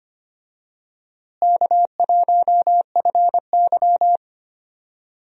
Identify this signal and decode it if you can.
Morse code, heard as a keyed tone, about 25 wpm, 700 Hz. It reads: K1FY
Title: USA Call Signs - 4 Characters - Repeated 2x - 25wpm